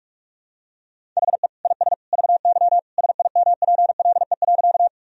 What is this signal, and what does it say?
Morse code, heard as a keyed tone, about 41 wpm, 700 Hz. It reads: HE IS VY SIMPLE.
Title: Sentences from Top 400 Words and Abbreviations 45wpm